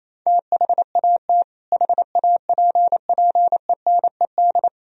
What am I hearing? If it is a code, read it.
Morse code, heard as a keyed tone, about 27 wpm, 700 Hz. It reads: THAT HAPPENED
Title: Sentences from Top 1000 Words - Review Part 2 - 28wpm